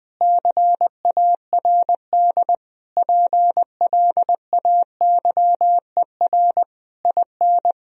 Morse code, heard as a keyed tone, about 21 wpm, 700 Hz. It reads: CARD PLAYER IN
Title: Sentences from Top 1000 Words - Review Part 4 - 20wpm